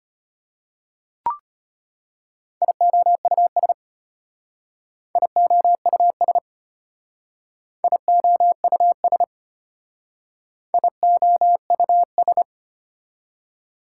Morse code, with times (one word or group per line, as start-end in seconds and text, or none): 2.6-3.7 IOUS
5.2-6.4 IOUS
7.8-9.2 IOUS
10.7-12.4 IOUS